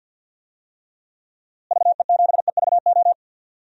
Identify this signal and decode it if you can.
Morse code, heard as a keyed tone, about 52 wpm, 700 Hz. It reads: VE7EVO
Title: DXCC-100 Call Signs - 6 Characters - Repeated 2x - 50wpm